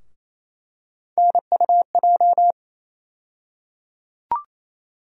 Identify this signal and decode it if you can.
Morse code, heard as a keyed tone, about 27 wpm, 700 Hz. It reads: NUJ
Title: USA Call Sign Postfixes - 3 Letters 28wpm